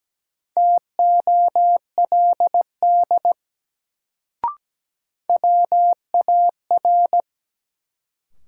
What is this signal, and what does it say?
Morse code, heard as a keyed tone, about 17 wpm, 700 Hz. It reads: TOLD WAR
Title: Sentences from Top 400 Words - New Word Drill - 17wpm